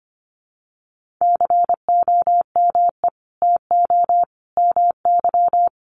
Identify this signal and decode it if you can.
Morse code, heard as a keyed tone, about 25 wpm, 700 Hz. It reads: COME TO MY
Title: Sentences from Top 1500 Words 25wpm